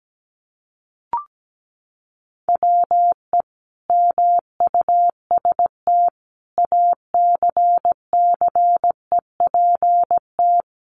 Morse code, heard as a keyed tone, about 17 wpm, 700 Hz. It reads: WE MUST ACCEPT
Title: Sentences from Top 1300 Words 17wpm